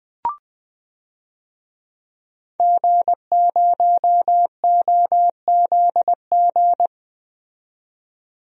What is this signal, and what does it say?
Morse code, heard as a keyed tone, about 20 wpm, 700 Hz. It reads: G0OZG
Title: DXCC-100 Call Signs - 5 Characters - 20wpm